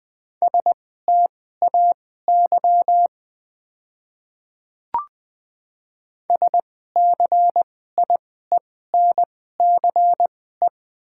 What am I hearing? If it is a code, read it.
Morse code, heard as a keyed tone, about 20 wpm, spaced as for 14 wpm, 700 Hz. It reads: STAY SCIENCE
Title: 2x Character Spacing - Top 500 Words - 20wpm